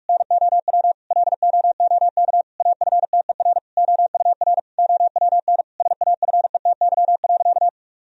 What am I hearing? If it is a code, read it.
Morse code, heard as a keyed tone, about 46 wpm, 700 Hz. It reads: NOW LOOK AFTER OUR OWN SAFETY.
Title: Sentences from Top 1000 Words - Review Part 4 - 45wpm